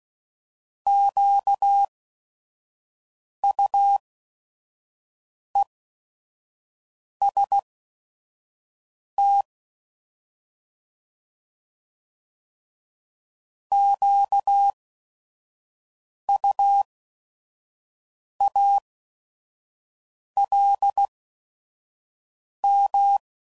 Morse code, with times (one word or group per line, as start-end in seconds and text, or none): 0.9-9.4 QUEST
13.7-23.2 QUALM